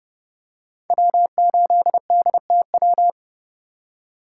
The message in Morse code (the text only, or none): W8DTW